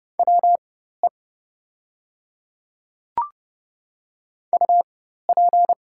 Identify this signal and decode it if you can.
Morse code, heard as a keyed tone, about 30 wpm, spaced as for 14 wpm, 700 Hz. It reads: WE UP
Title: 4x Character Spacing - Top 300 Words - 30wpm